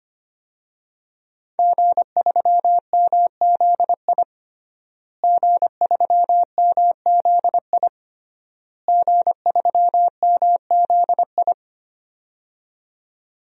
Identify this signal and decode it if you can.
Morse code, heard as a keyed tone, about 25 wpm, 700 Hz. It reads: G3MZI G3MZI G3MZI
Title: DXCC-100 Call Signs - 5 Characters - Repeated 3x - 25wpm